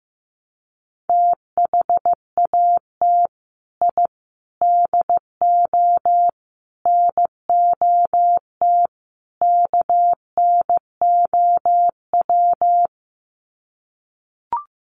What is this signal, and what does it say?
Morse code, heard as a keyed tone, about 14 wpm, 700 Hz. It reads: THAT I DO NOT KNOW
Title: Sets of 5 Words Encore2 15wpm